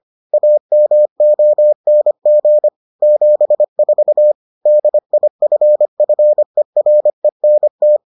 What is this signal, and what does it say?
Morse code, heard as a keyed tone, about 25 wpm, 600 Hz. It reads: AMONG 74 DIFFERENT